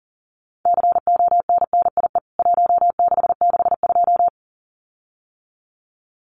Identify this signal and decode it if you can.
Morse code, heard as a keyed tone, about 40 wpm, 700 Hz. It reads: CONNIE 1662
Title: CWT Exchange 40wpm